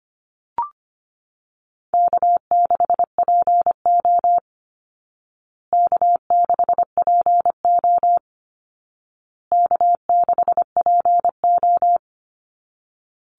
Morse code, heard as a keyed tone, about 25 wpm, 700 Hz. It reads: K6PO K6PO K6PO
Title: Call Signs Repeated 2x - 25wpm